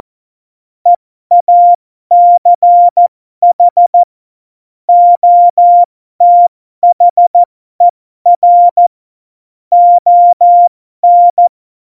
Morse code, heard as a keyed tone, about 14 wpm, spaced as for 12 wpm, 700 Hz. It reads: EACH OTHER ON
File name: Morsekurs / Witze